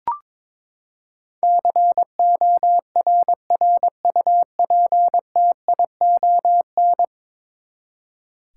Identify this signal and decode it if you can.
Morse code, heard as a keyed tone, about 22 wpm, 700 Hz. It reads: CORRUPTION
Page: morse-code-ninja-practice-22wpm